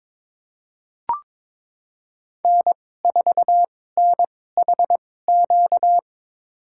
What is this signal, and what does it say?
Morse code, heard as a keyed tone, about 21 wpm, spaced as for 16 wpm, 700 Hz. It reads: N4NHQ
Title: USA Call Signs - 5 Characters - 2x Character Spacing - 22wpm